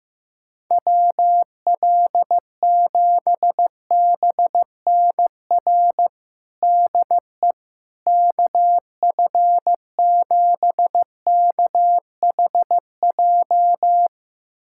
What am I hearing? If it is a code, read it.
Morse code, heard as a keyed tone, about 16 wpm, 700 Hz. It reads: WL7BNR DE KF7KHJ
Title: USA Callsign DE Callsign - 6 Characters - 15wpm